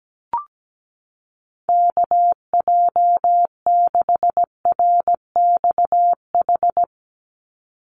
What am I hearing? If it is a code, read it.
Morse code, heard as a keyed tone, about 17 wpm, 700 Hz. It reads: KJ6RXH
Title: USA Call Signs 17wpm